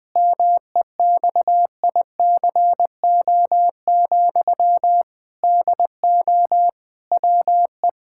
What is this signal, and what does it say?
Morse code, heard as a keyed tone, about 21 wpm, 700 Hz. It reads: MEXICO, DO WE
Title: Pangram Challenge v2 - 20wpm